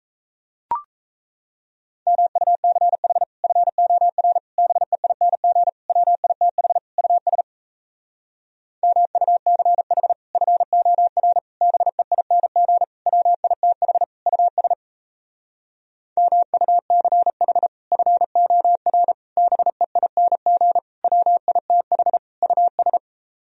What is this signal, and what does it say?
Morse code, heard as a keyed tone, about 38 wpm, 700 Hz. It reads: MUCH FOR BEING WITH US MUCH FOR BEING WITH US MUCH FOR BEING WITH US